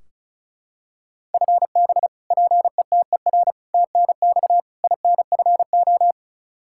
Morse code, heard as a keyed tone, about 34 wpm, 700 Hz. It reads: FB PETER TNX INFO
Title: Practice QSO - Exchange 3 - More Info - 35wpm